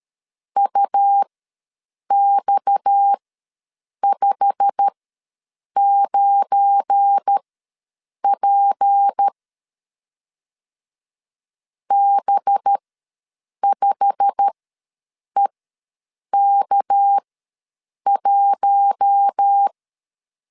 Morse code, heard as a keyed tone, about 13 wpm, spaced as for 7 wpm, 800 Hz. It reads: UX59P B5EK1
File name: LEZIONI / 8 YJ45 / 8 YJ45 ws 12wpm 3146